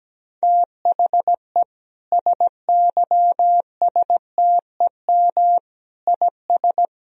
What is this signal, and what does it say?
Morse code, heard as a keyed tone, about 17 wpm, 700 Hz. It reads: THE SYSTEM IS
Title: Sentences from Top 1000 Words - Review Part 3 - 17wpm